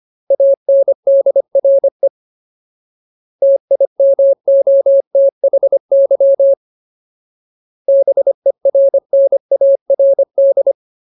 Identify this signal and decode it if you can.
Morse code, heard as a keyed tone, about 25 wpm, 550 Hz. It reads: ANDRE TIMOTHY BERNARD
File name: Ultimate Word List / 25WPM Ultimate